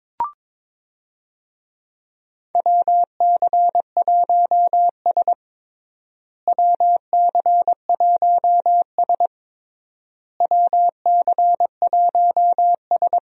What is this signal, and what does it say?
Morse code, heard as a keyed tone, about 22 wpm, 700 Hz. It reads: WC1S WC1S WC1S
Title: USA Call Signs - 4 Characters - Repeated 3x - 22wpm